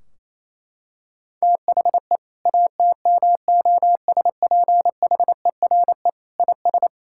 Morse code, heard as a keyed tone, about 28 wpm, 700 Hz. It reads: THE ATMOSPHERE IS
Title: Sentences from Top 1600 Words 28wpm